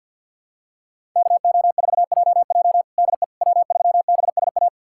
Morse code, heard as a keyed tone, about 53 wpm, 700 Hz. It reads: KO4JJ DE W3BSA
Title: USA Callsign DE Callsign - 5 Characters - 50wpm